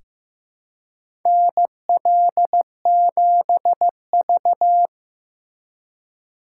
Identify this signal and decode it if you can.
Morse code, heard as a keyed tone, about 15 wpm, 700 Hz. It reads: NL7V
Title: K1USN SST QSO Practice - 15wpm